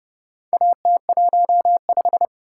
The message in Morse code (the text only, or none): AT15